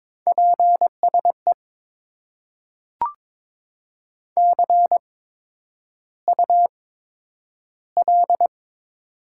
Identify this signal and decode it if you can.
Morse code, heard as a keyed tone, about 21 wpm, spaced as for 9 wpm, 700 Hz. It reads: PSE CUL
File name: Top 90 QSO Elements - 8x Character Spacing - 22wpm